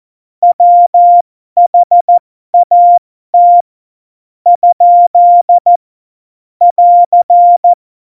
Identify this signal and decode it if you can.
Morse code, heard as a keyed tone, about 14 wpm, spaced as for 10 wpm, 700 Hz. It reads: WHAT? <AR>